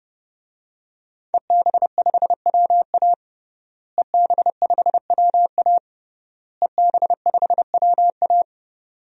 Morse code, heard as a keyed tone, about 32 wpm, 700 Hz. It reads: EB5WA EB5WA EB5WA
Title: DXCC-100 Call Signs - 5 Characters - Repeated 3x - 30wpm